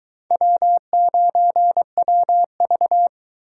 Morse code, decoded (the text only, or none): W9WV